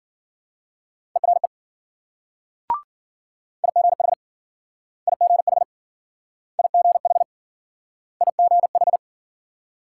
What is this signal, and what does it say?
Morse code, heard as a keyed tone, about 53 wpm, 700 Hz. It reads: ERE IGH IGH IGH IGH